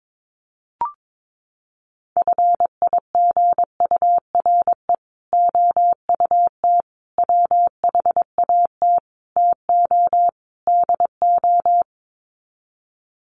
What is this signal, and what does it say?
Morse code, heard as a keyed tone, about 22 wpm, 700 Hz. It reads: FIGURE OUT WHAT TO DO
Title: Sets of 5 Words Encore2 22wpm